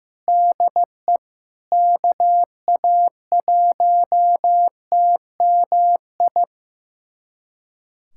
Morse code, heard as a keyed tone, about 15 wpm, 700 Hz. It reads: DE KA1TMI
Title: USA Callsign DE Callsign - 6 Characters - 15wpm